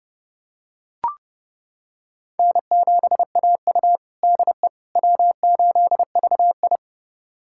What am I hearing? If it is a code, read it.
Morse code, heard as a keyed tone, about 30 wpm, 700 Hz. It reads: N7AU DE W8VI
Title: USA Callsign DE Callsign - 4 Characters - 30wpm